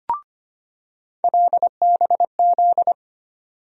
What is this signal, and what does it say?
Morse code, heard as a keyed tone, about 25 wpm, 700 Hz. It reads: LBZ
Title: USA Call Sign Postfixes - 3 Letters 25wpm